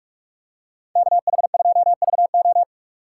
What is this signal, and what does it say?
Morse code, heard as a keyed tone, about 46 wpm, 700 Hz. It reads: KH2VO
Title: USA Call Signs - 6x Character Spacing - 45wpm